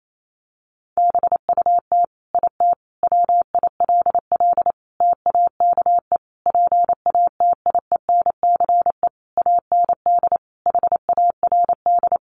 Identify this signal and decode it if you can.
Morse code, heard as a keyed tone, about 28 wpm, 700 Hz. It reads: BUT IT WILL TAKE PATIENCE AND HARD
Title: Sentences from Top 1300 Words 28wpm